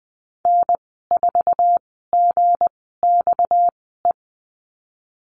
Morse code, heard as a keyed tone, about 20 wpm, 700 Hz. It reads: N 4 G X E
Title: 238 Lesson 38g - Letter q - Call Signs 20wpm